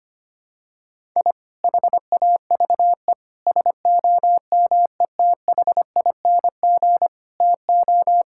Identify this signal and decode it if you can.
Morse code, heard as a keyed tone, about 25 wpm, 700 Hz. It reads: I HAVE SOMETHING TO
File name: practice / 25wpm / Sentences with 2-Word Phrases - 25wpm